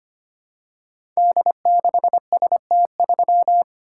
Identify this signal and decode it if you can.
Morse code, heard as a keyed tone, about 25 wpm, 700 Hz. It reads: D6ST3
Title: Code Groups - 5 Letter:Numbers - 25wpm